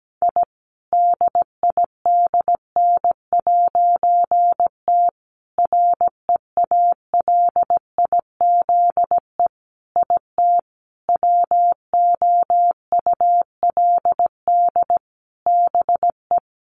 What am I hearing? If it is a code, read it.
Morse code, heard as a keyed tone, about 17 wpm, 700 Hz. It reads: I DIDN'T REALIZE IT WOULD BE
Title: A Sound Of Thunder 17wpm